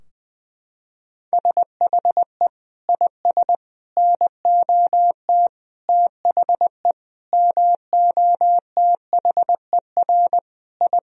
Morse code, heard as a keyed tone, about 20 wpm, 700 Hz. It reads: SHE IS NOT THE MOTHER I
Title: Sentences from Top 1000 Words - Review Part 2 - 20wpm